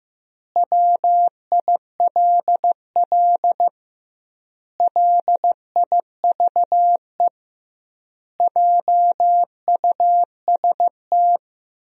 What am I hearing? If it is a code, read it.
Morse code, heard as a keyed tone, about 15 wpm, 700 Hz. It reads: WILL LIVE JUST
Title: 2x Word-Spacing - Sentences from Top 300 Words - 15wpm